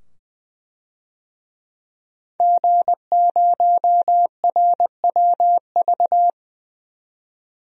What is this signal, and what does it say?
Morse code, heard as a keyed tone, about 20 wpm, 700 Hz. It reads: G0RWV
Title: DXCC-100 Call Signs - 5 Characters - 20wpm